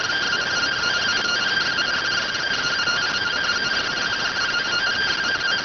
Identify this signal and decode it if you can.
Two-tone teleprinter signal, about 170 Hz shift.